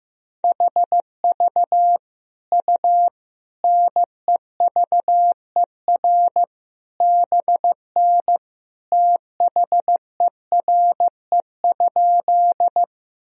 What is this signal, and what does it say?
Morse code, heard as a keyed tone, about 15 wpm, 700 Hz. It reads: HV U NEVER BN THERE?
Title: Sentences from Top 300 Words and Abbreviations 15wpm